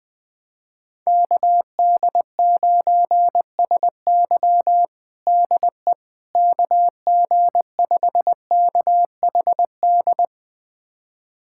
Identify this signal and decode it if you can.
Morse code, heard as a keyed tone, about 20 wpm, 700 Hz. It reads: KD9SY DE KG5KHD
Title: USA Callsign DE Callsign 20wpm